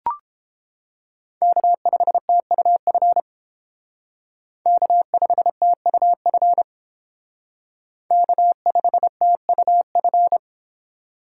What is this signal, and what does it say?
Morse code, heard as a keyed tone, about 29 wpm, 700 Hz. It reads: K5TUF K5TUF K5TUF